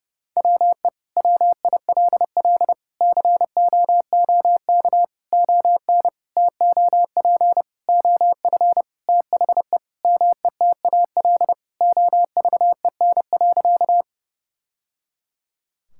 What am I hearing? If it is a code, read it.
Morse code, heard as a keyed tone, about 30 wpm, 700 Hz. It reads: WE WILL COOK ON TOP OF THE METAL OVEN.
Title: Sentences from Top 1000 Words - Review Part 6 - 30wpm